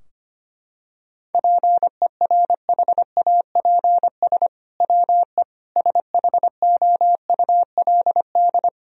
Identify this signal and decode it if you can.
Morse code, heard as a keyed tone, about 25 wpm, 700 Hz. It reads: PERHAPS WE SHOULD
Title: Sentences from Top 1000 Words - Review Part 4 - 25wpm